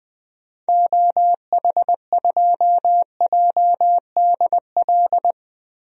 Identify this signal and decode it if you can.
Morse code, heard as a keyed tone, about 20 wpm, 700 Hz. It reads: OH2JDL
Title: DXCC-100 Call Signs - 6 Characters - Repeated 2x - 20wpm